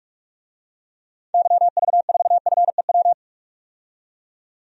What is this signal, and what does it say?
Morse code, heard as a keyed tone, about 47 wpm, 700 Hz. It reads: YV4FEW
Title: DXCC-100 Call Signs - 6 Characters - 45wpm